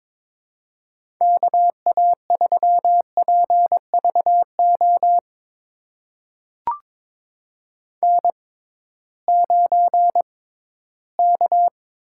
Morse code, heard as a keyed tone, about 22 wpm, spaced as for 14 wpm, 700 Hz. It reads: KA3PVO N9K
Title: USA Call Signs - 6x Character Spacing - 22wpm